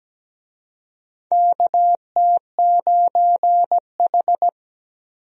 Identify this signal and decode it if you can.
Morse code, heard as a keyed tone, about 17 wpm, 700 Hz. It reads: KT9H
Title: USA Call Signs - 4 Characters - Repeated 3x - 17wpm